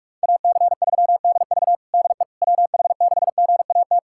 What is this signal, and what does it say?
Morse code, heard as a keyed tone, about 46 wpm, 700 Hz. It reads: AC3DV DE WH6GAT
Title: USA Callsign DE Callsign 45wpm